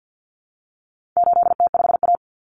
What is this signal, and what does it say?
Morse code, heard as a keyed tone, about 49 wpm, 700 Hz. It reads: 8T5A
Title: Common Cut Numbers - 4-Digit 50wpm